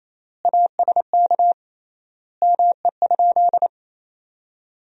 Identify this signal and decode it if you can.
Morse code, heard as a keyed tone, about 27 wpm, 700 Hz. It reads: ASK ME?
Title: 3x Word-Spacing - Sentences from Top 200 Words - 28wpm